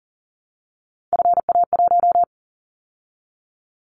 Morse code, heard as a keyed tone, about 40 wpm, 700 Hz. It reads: FA1
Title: Callsign Prefixes for Top 50 DXCC - 40wpm